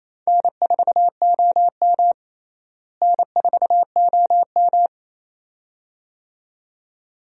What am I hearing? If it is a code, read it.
Morse code, heard as a keyed tone, about 28 wpm, 700 Hz. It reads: N4OM N4OM
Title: USA Call Signs - 4 Characters - Repeated 2x - 28wpm